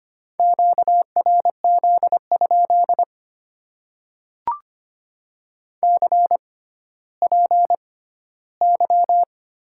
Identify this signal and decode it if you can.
Morse code, heard as a keyed tone, about 25 wpm, spaced as for 14 wpm, 690 Hz. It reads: QRZ? CPY
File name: Top 90 QSO Elements - 6x Character Spacing - 25wpm